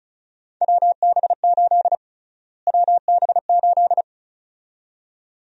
Morse code, heard as a keyed tone, about 36 wpm, 700 Hz. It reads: WB8 WB8
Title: USA Call Sign Prefixes - Repeated 2x - 35wpm